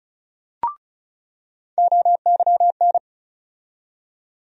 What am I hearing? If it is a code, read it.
Morse code, heard as a keyed tone, about 34 wpm, 700 Hz. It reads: OYN
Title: USA Call Sign Postfixes - 3 Letters 35wpm